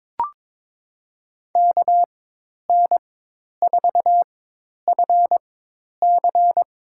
Morse code, heard as a keyed tone, about 22 wpm, spaced as for 10 wpm, 700 Hz. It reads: KN4FC
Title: USA Call Signs - 4x Character Spacing - 22wpm